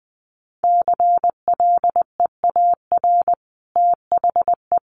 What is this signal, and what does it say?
Morse code, heard as a keyed tone, about 19 wpm, 700 Hz. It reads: CLEAR THE
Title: Sentences from Top 1000 Words 20wpm